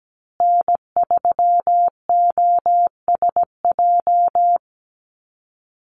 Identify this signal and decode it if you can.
Morse code, heard as a keyed tone, about 17 wpm, 700 Hz. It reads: N3OSJ
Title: USA Call Signs Repeated 2x 17wpm